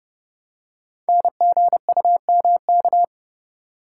Morse code, heard as a keyed tone, about 30 wpm, 700 Hz. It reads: NGUMK
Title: Code Groups - 5 Letters - 30wpm